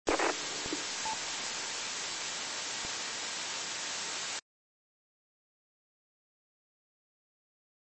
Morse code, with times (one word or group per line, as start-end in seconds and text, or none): none